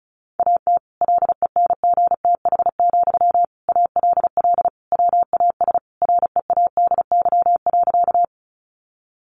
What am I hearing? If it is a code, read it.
Morse code, heard as a keyed tone, about 34 wpm, 700 Hz. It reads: AT LENGTH, ALL WAS READY.
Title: Sentences from Top 1000 Words 35wpm